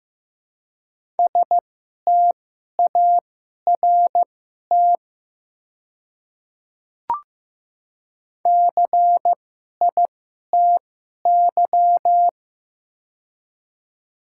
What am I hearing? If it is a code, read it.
Morse code, heard as a keyed tone, about 16 wpm, spaced as for 11 wpm, 700 Hz. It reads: START CITY
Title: 2x Character Spacing - Top 300 Words - 15wpm